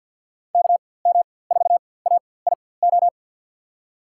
Morse code, heard as a keyed tone, about 48 wpm, spaced as for 23 wpm, 700 Hz. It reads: KM4AIO
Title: USA Call Signs - 4x Character Spacing - 50wpm